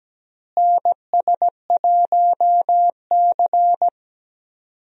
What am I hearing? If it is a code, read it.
Morse code, heard as a keyed tone, about 17 wpm, 700 Hz. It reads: NS1C
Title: USA Call Signs - 4 Characters - Repeated 3x - 17wpm